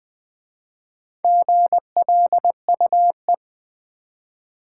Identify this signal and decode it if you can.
Morse code, heard as a keyed tone, about 20 wpm, 700 Hz. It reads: GLUE
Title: Easy Words with Hints - 20wpm